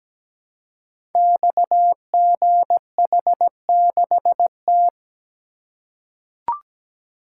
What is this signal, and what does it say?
Morse code, heard as a keyed tone, about 17 wpm, 700 Hz. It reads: XGH6T
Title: Code Groups - 5 Letter:Numbers - 17wpm